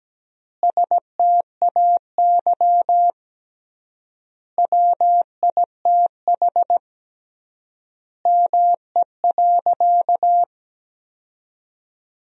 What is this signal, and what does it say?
Morse code, heard as a keyed tone, about 17 wpm, 700 Hz. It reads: STAY WITH ME.
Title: 3x Word-Spacing - Sentences from Top 500 Words - 17wpm